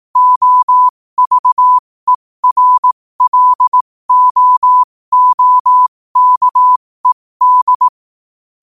Morse code, heard as a keyed tone, about 18 wpm, spaced as for 15 wpm, 1000 Hz. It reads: OVERLOOKED
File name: qotdfiles / 15wpm